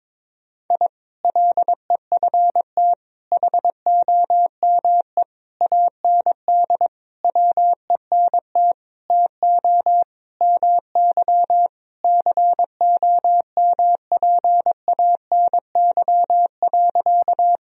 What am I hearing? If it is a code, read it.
Morse code, heard as a keyed tone, about 22 wpm, 700 Hz. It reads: I LEFT HOME AND WENT TO MY COMPANY.
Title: Sentences from Top 1000 Words - Review Part 3 - 22wpm